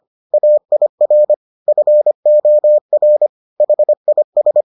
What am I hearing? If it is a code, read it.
Morse code, heard as a keyed tone, about 25 wpm, 600 Hz. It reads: AIR FOR HIS